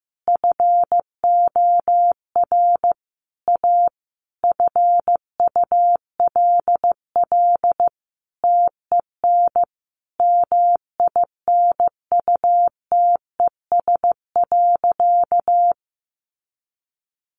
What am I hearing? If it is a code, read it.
Morse code, heard as a keyed tone, about 15 wpm, 700 Hz. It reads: FOR A FULL TEN MINUTES.
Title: A Sound Of Thunder 15wpm